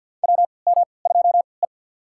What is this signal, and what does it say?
Morse code, heard as a keyed tone, about 51 wpm, spaced as for 29 wpm, 700 Hz. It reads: WM2E
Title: USA Call Signs - 4 Characters - 3x Character Spacing - 50wpm